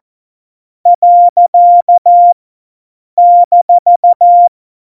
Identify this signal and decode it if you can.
Morse code, heard as a keyed tone, about 14 wpm, 700 Hz. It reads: . -
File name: Morsekurs / Witze